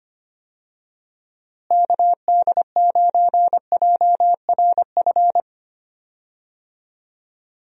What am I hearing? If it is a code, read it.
Morse code, heard as a keyed tone, about 25 wpm, 700 Hz. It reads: KD9JRF
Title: USA Call Signs - 6 Characters - Repeated 3x - 25wpm